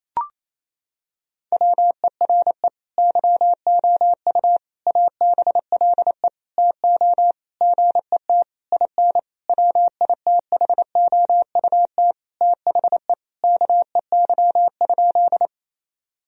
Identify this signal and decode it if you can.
Morse code, heard as a keyed tone, about 28 wpm, 700 Hz. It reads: WERE YOU ABLE TO GET IN WITHOUT THE KEY?
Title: Sentences from Top 1000 Words - Review Part 4 - 28wpm